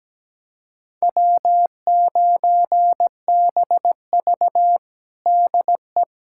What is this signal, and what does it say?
Morse code, heard as a keyed tone, about 17 wpm, 700 Hz. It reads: W9BV DE
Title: USA Callsign DE Callsign - 4 Characters - 17wpm